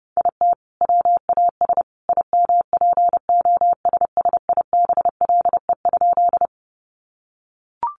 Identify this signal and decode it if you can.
Morse code, heard as a keyed tone, about 30 wpm, 700 Hz. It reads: IT WAS IMPOSSIBLE?
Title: Sentences from Top 1100 Words 30wpm